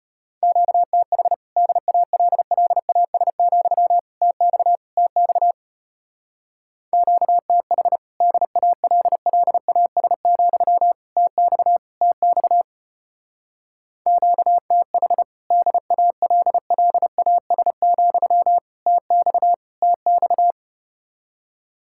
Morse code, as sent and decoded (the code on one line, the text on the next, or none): --.- - .... / -.. .- .-.. .-.. .- ... --..-- / - -..- / - -..- / --.- - .... / -.. .- .-.. .-.. .- ... --..-- / - -..- / - -..- / --.- - .... / -.. .- .-.. .-.. .- ... --..-- / - -..- / - -..-
QTH DALLAS, TX TX QTH DALLAS, TX TX QTH DALLAS, TX TX